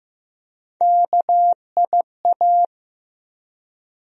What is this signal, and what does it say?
Morse code, heard as a keyed tone, about 15 wpm, 700 Hz. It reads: KIA
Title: USA Call Sign Postfixes - 3 Letters 15wpm